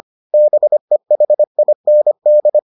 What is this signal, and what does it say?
Morse code, heard as a keyed tone, about 25 wpm, 600 Hz. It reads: BEHIND